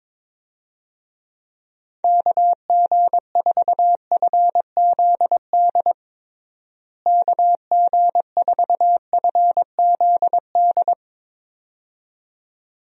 Morse code, heard as a keyed tone, about 22 wpm, 700 Hz. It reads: KG4FZD KG4FZD
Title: USA Call Signs - 6 Characters - Repeated 2x - 22wpm